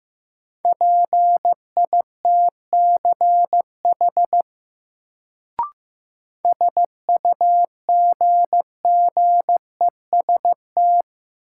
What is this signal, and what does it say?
Morse code, heard as a keyed tone, about 15 wpm, 700 Hz. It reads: PITCH SUGGEST
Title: Sentences from Top 1300 Words - New Word Drill Alt - 15wpm